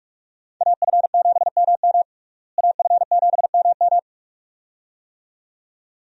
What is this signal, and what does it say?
Morse code, heard as a keyed tone, about 41 wpm, 700 Hz. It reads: AF7MM AF7MM
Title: USA Call Signs - 5 Characters - Repeated 2x - 45wpm